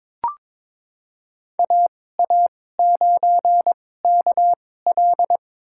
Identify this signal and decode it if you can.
Morse code, heard as a keyed tone, about 22 wpm, spaced as for 16 wpm, 700 Hz. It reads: AA9KL
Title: USA Call Signs - 5 Characters - 2x Character Spacing - 22wpm